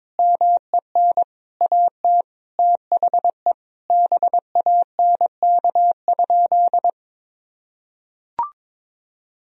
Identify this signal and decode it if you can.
Morse code, heard as a keyed tone, about 22 wpm, 700 Hz. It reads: MEN AT THE BANK?
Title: Sentences from Top 1000 Words - Review Part 4 - 22wpm